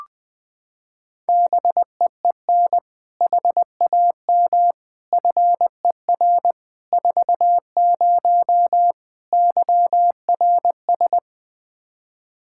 Morse code, heard as a keyed tone, about 21 wpm, 700 Hz. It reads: BEEN HAM FER 40 YRS
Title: Practice QSO - Exchange 3 - More Info - 20wpm